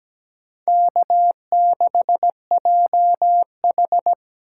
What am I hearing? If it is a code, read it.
Morse code, heard as a keyed tone, about 20 wpm, 700 Hz. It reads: K6JH
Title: Speed Racing - USA 1x2 Call Signs 17wpm